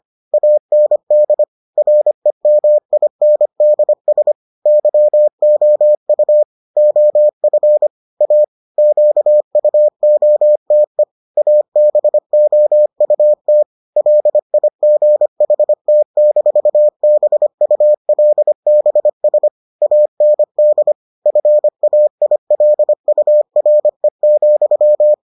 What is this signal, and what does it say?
Morse code, heard as a keyed tone, about 25 wpm, 600 Hz. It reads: AND REMINDS YOU OF A QUOTE ABOUT LIGHT-BULBS AND FAILURE,